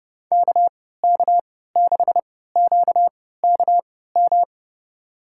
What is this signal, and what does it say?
Morse code, heard as a keyed tone, about 30 wpm, spaced as for 17 wpm, 700 Hz. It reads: KK6QKM
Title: USA Call Signs - 6 Characters - 3x Character Spacing - 30wpm